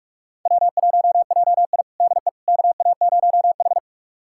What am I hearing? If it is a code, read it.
Morse code, heard as a keyed tone, about 45 wpm, 700 Hz. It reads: W1JI DE KA0H
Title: USA Callsign DE Callsign - 4 Characters - 45wpm